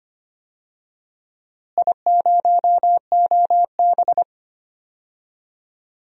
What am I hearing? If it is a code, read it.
Morse code, heard as a keyed tone, about 25 wpm, 700 Hz. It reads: I0OB
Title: DXCC-100 Call Signs - 4 Characters - Repeated 3x - 25wpm